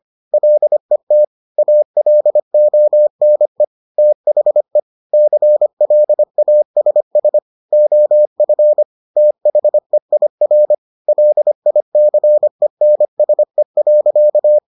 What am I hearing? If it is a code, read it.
Morse code, heard as a keyed tone, about 25 wpm, 600 Hz. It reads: LET ALONE THE CLASS OF THEIR LICENSE.